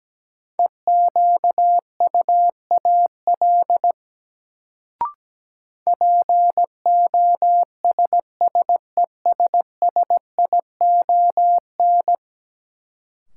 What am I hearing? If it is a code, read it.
Morse code, heard as a keyed tone, about 17 wpm, 700 Hz. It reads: EQUAL POSSESSION
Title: Sentences from Top 1400 Words - New Word Drill - 17wpm